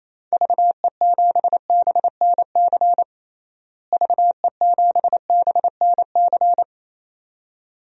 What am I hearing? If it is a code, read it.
Morse code, heard as a keyed tone, about 28 wpm, 700 Hz. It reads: VE7BNC VE7BNC
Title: DXCC-100 Call Signs - 6 Characters - Repeated 2x - 28wpm